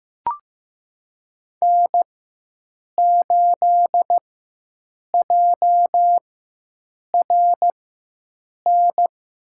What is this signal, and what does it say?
Morse code, heard as a keyed tone, about 15 wpm, spaced as for 7 wpm, 700 Hz. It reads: N8JRN
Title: USA Call Signs - 5 Characters - 4x Character Spacing - 15wpm